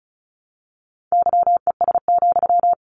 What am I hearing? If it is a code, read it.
Morse code, heard as a keyed tone, about 34 wpm, 700 Hz. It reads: YES,
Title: Sentences from Top 700 Words 35wpm